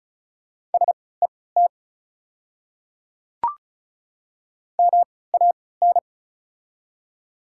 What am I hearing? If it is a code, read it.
Morse code, heard as a keyed tone, about 36 wpm, spaced as for 20 wpm, 700 Hz. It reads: SET MAN
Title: 3x Character Spacing - Top 200 Words - 35wpm